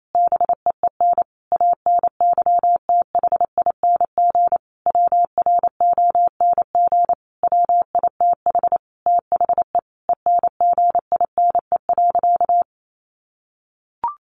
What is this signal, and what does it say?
Morse code, heard as a keyed tone, about 28 wpm, 700 Hz. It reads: BEEN ANYTHING WRONG WITH THE ENGINE.
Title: Sentences from Top 1500 Words 28wpm